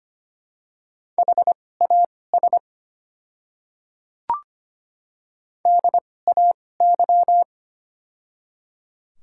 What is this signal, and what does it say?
Morse code, heard as a keyed tone, about 25 wpm, spaced as for 18 wpm, 700 Hz. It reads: HAS DAY
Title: 2x Character Spacing - Top 100 Words - 25wpm